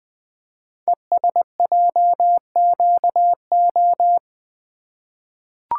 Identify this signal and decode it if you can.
Morse code, heard as a keyed tone, about 20 wpm, 700 Hz. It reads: ESJQO
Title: Code Groups - 5 Letters - 20wpm